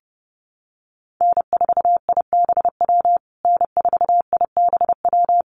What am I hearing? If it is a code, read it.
Morse code, heard as a keyed tone, about 28 wpm, 700 Hz. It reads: N4IBW N4IBW
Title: USA Call Signs Repeated 3x 30wpm